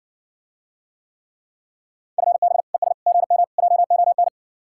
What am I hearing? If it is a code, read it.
Morse code, heard as a keyed tone, about 60 wpm, 700 Hz. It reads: FB ES GM JON